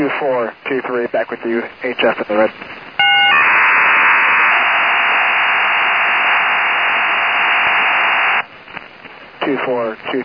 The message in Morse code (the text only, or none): none